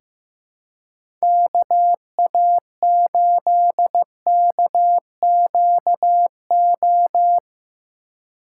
Morse code, heard as a keyed tone, about 15 wpm, 700 Hz. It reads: KA8KQO
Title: USA Call Signs - 6 Characters - 6x Character Spacing - 15wpm